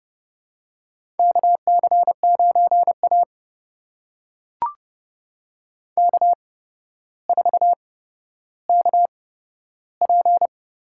Morse code, heard as a keyed tone, about 30 wpm, spaced as for 13 wpm, 700 Hz. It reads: KC9A K4KP
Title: USA Call Signs - 4 Characters - 8x Character Spacing - 30wpm